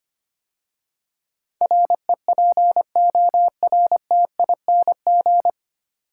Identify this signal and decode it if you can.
Morse code, heard as a keyed tone, about 25 wpm, 700 Hz. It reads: REPORTING
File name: practice / 25wpm / 9 Letter Words - First 3 Letters Given - 25wpm